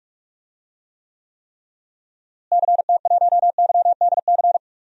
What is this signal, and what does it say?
Morse code, heard as a keyed tone, about 41 wpm, 700 Hz. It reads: CT1YDC
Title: DXCC-100 Call Signs - 6 Characters - Repeated 3x - 45wpm